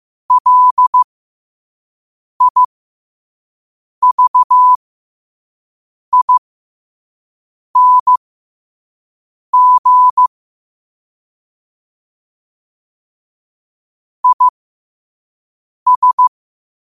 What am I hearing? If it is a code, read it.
Morse code, heard as a keyed tone, about 15 wpm, spaced as for 5 wpm, 1000 Hz. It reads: LIVING IS
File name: qotdfiles / 5wpm